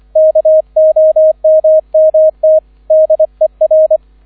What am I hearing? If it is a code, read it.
Morse code, heard as a keyed tone, about 25 wpm, 630 Hz, with radio noise. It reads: KOMMT DER